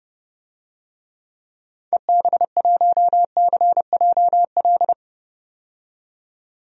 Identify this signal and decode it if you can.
Morse code, heard as a keyed tone, about 32 wpm, 700 Hz. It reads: EB1CJL
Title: DXCC-100 Call Signs - 6 Characters - 30wpm